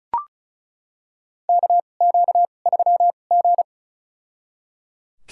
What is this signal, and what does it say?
Morse code, heard as a keyed tone, about 34 wpm, spaced as for 25 wpm, 700 Hz. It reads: KQ3G
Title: USA Call Signs - 2x Character Spacing - 35wpm